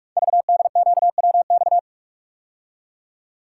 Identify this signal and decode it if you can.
Morse code, heard as a keyed tone, about 44 wpm, 690 Hz. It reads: VDQWX
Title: Code Groups - 5 Letter:Numbers - 45wpm